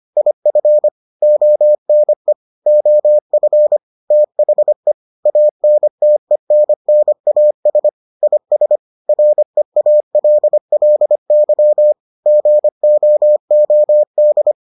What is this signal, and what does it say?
Morse code, heard as a keyed tone, about 25 wpm, 600 Hz. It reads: IF ONE OF THE ANTENNAS IS REALLY GOOD